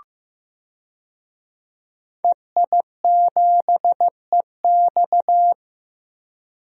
Morse code, heard as a keyed tone, about 15 wpm, 700 Hz. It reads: EI7EX
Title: DXCC-100 Call Signs - 5 Characters - Repeated 3x - 15wpm